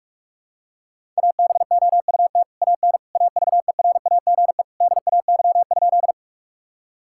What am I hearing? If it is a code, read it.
Morse code, heard as a keyed tone, about 42 wpm, 700 Hz. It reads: ABOUT AN AVERAGE DAY?
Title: Sentences from Top 1000 Words - Review Part 6 - 45wpm